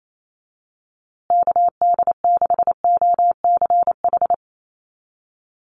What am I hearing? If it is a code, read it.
Morse code, heard as a keyed tone, about 28 wpm, 690 Hz. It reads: KD6OCH
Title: USA Call Signs Repeated 3x 28wpm